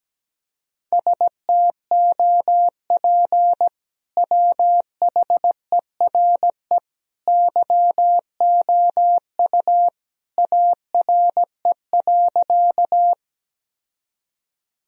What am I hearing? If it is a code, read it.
Morse code, heard as a keyed tone, about 17 wpm, 700 Hz. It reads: STOP WHERE YOU ARE.
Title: Sentences from Top 1000 Words - Review Part 1 - 17wpm